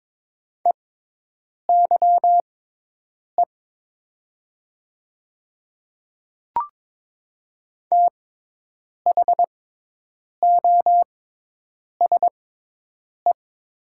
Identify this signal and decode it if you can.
Morse code, heard as a keyed tone, about 22 wpm, spaced as for 8 wpm, 700 Hz. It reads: EYE THOSE
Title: 6x Character Spacing - Top 300 Words - 22wpm